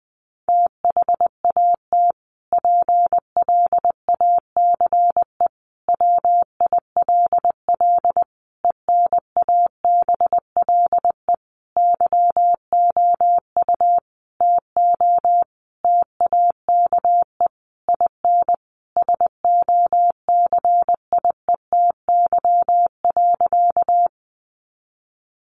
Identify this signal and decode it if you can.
Morse code, heard as a keyed tone, about 20 wpm, 700 Hz. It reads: THAT PLACE WILL ENABLE YOU TO TAKE IN SOCIETY.
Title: Sentences from Top 1500 Words 20wpm